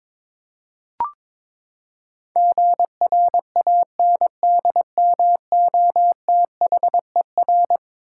Morse code, heard as a keyed tone, about 22 wpm, 700 Hz. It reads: GRANDMOTHER
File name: practice / 22wpm / Eleven Letter Words - 22wpm